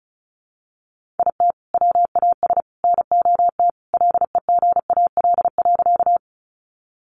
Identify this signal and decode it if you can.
Morse code, heard as a keyed tone, about 34 wpm, 700 Hz. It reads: IT WAS NOT LEGAL.
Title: Sentences from Top 1200 Words 35wpm